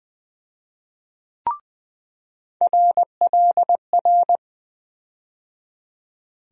Morse code, heard as a keyed tone, about 20 wpm, 700 Hz. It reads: RLR
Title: USA Call Sign Postfixes - 3 Letters 20wpm